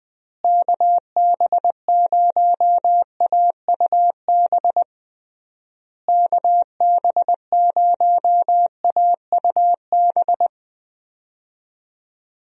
Morse code, heard as a keyed tone, about 20 wpm, 700 Hz. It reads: KB0AUB KB0AUB
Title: USA Call Signs - 6 Characters - Repeated 3x - 20wpm